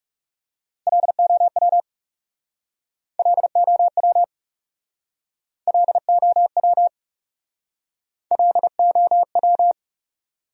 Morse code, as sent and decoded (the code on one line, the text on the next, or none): .-.. --- .-- / .-.. --- .-- / .-.. --- .-- / .-.. --- .--
LOW LOW LOW LOW